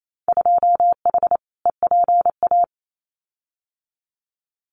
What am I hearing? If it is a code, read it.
Morse code, heard as a keyed tone, about 28 wpm, 700 Hz. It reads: 2H EPA
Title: Winter Field Day Exchanges Encore 28wpm